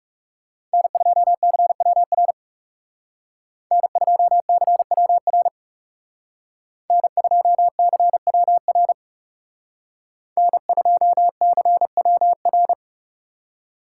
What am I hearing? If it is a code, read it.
Morse code, heard as a keyed tone, about 36 wpm, 700 Hz. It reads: N2CWR N2CWR N2CWR N2CWR
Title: Speed Racing - USA 1x3 Call Signs 30wpm